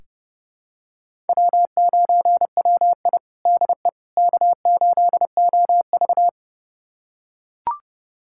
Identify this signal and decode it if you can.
Morse code, heard as a keyed tone, about 30 wpm, 700 Hz. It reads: W9WI DE K8OV
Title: USA Callsign DE Callsign - 4 Characters - 30wpm